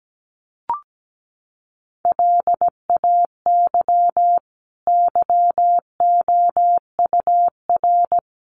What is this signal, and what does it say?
Morse code, heard as a keyed tone, about 17 wpm, 700 Hz. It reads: LAY YOUR
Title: Sentences from Top 400 Words 17wpm